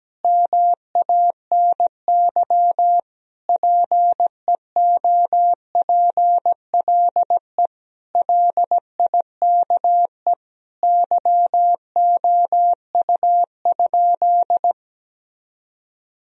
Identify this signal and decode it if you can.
Morse code, heard as a keyed tone, about 17 wpm, 700 Hz. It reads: MANY PEOPLE LIKE YOU?
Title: Sentences from Top 1000 Words - Review Part 1 - 17wpm